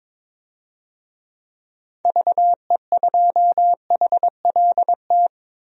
Morse code, heard as a keyed tone, about 22 wpm, 700 Hz. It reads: VE2HLT
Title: DXCC-100 Call Signs - 6 Characters - 22wpm